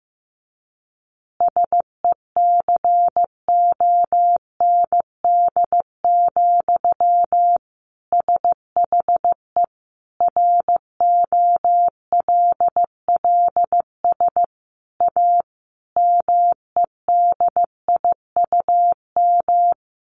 Morse code, heard as a keyed tone, about 15 wpm, 700 Hz. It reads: SECOND, SHE ROLLS A MEDIUM
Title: Easy Phrases 15wpm